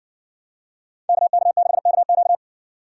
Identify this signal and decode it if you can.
Morse code, heard as a keyed tone, about 62 wpm, 700 Hz. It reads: KK6KQ